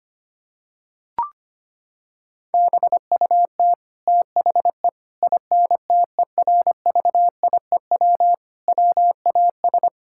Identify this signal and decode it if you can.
Morse code, heard as a keyed tone, about 25 wpm, 700 Hz. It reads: BUT THE INTERVIEW WAS
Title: Sentences from Top 1600 Words 25wpm